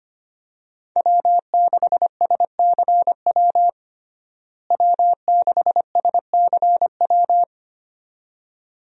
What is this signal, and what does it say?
Morse code, heard as a keyed tone, about 25 wpm, 700 Hz. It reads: W6SCW W6SCW
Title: USA Call Signs - 5 Characters - Repeated 2x - 25wpm